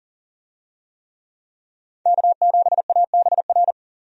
Morse code, heard as a keyed tone, about 39 wpm, 700 Hz. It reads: K7ABR
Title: USA Call Signs - 5 Characters - 40wpm